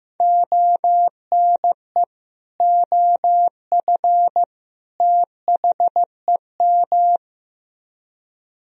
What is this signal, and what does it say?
Morse code, heard as a keyed tone, about 15 wpm, 700 Hz. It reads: ONE OF THEM
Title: Speed-Racing - Sets of 3 Words 15wpm